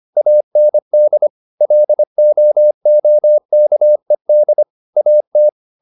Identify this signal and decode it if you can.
Morse code, heard as a keyed tone, about 25 wpm, 600 Hz. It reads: AND LOOKED AT